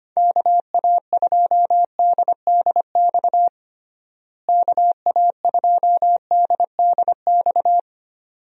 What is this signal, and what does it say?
Morse code, heard as a keyed tone, about 25 wpm, 700 Hz. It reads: KA2DDX KA2DDX
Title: USA Call Signs - 6 Characters - Repeated 2x - 25wpm